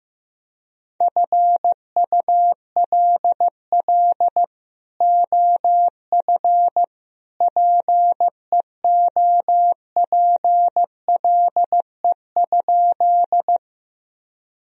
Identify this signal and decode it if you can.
Morse code, heard as a keyed tone, about 16 wpm, 700 Hz. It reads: FULL OF PEOPLE?
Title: Sentences from Top 1000 Words - Review Part 3 - 15wpm